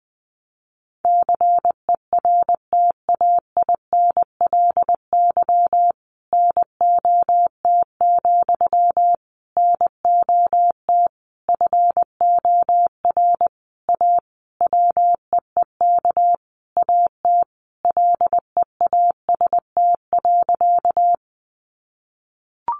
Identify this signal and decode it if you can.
Morse code, heard as a keyed tone, about 19 wpm, 700 Hz. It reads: CERTAINLY NOT, NOT FOR A WEEK AT LEAST.
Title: Sentences from Top 900 Words 20wpm